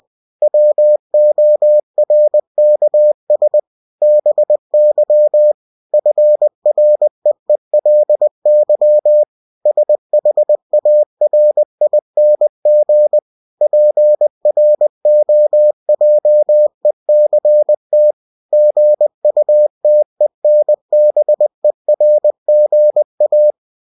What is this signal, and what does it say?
Morse code, heard as a keyed tone, about 20 wpm, 600 Hz. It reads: WORKS BY FREELY SHARING PROJECT GUTENBERG A
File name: cw-books / winnie-the-pooh / winnie-20wpm-page-0114